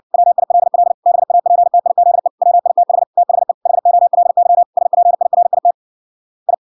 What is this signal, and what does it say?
Morse code, heard as a keyed tone, about 58 wpm, 700 Hz. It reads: WILL BAPTIZE WITH THE HOLY SPIRIT I